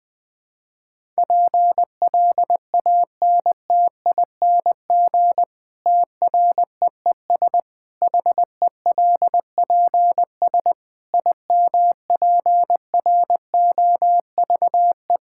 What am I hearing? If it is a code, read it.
Morse code, heard as a keyed tone, about 20 wpm, 700 Hz. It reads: PLANTING TREES HELPS IMPROVE
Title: Difficult Sentences with Hints - 20wpm